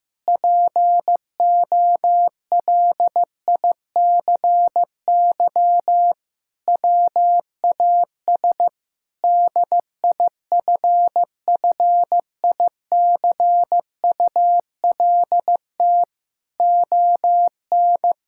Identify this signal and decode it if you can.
Morse code, heard as a keyed tone, about 15 wpm, 700 Hz. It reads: POLICY WAS DIFFICULT ON
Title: Sentences from Top 1000 Words - Review Part 5 - 15wpm